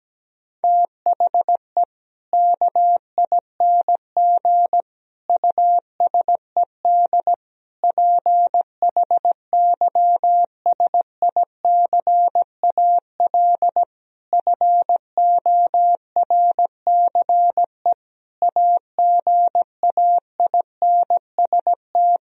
Morse code, heard as a keyed tone, about 17 wpm, 700 Hz. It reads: THE KING USED PHYSICAL FORCE AGAINST
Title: Sentences from Top 1000 Words - Review Part 4 - 17wpm